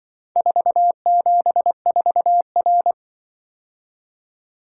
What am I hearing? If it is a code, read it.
Morse code, heard as a keyed tone, about 24 wpm, 700 Hz. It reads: 474R
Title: Speed-Racing - Four Letter-Number Combinations - 20wpm